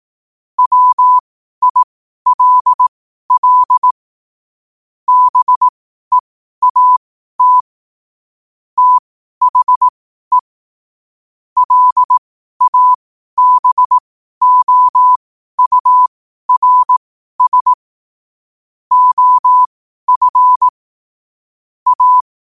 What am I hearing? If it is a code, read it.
Morse code, heard as a keyed tone, about 18 wpm, spaced as for 12 wpm, 1000 Hz. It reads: WILL BEAT THE LABOURS OF A
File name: qotdfiles / 12wpm